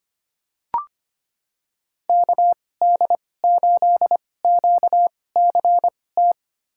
Morse code, heard as a keyed tone, about 25 wpm, spaced as for 18 wpm, 700 Hz. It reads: KD8QCT
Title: USA Call Signs - 6 Characters - 2x Character Spacing - 25wpm